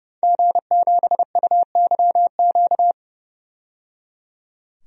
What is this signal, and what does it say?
Morse code, heard as a keyed tone, about 30 wpm, 700 Hz. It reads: G7UYQ
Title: DXCC-100 Call Signs - 5 Characters - 30wpm